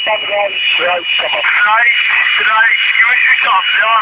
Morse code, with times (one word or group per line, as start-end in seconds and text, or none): none